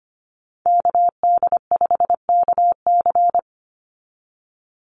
Morse code, heard as a keyed tone, about 25 wpm, 700 Hz. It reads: KD5KC
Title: NA SOTA Activator Callsigns 25wpm